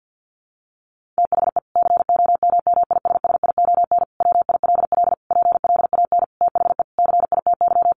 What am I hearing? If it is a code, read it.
Morse code, heard as a keyed tone, about 50 wpm, 700 Hz. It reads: THE COMMISSION WILL PLAN THE CITY
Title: Sentences from Top 1400 Words 50wpm